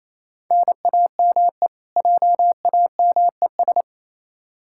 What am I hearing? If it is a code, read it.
Morse code, heard as a keyed tone, about 28 wpm, 700 Hz. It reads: NAME JAMES
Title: Practice QSO - Exchange 1 - Beginning - 28wpm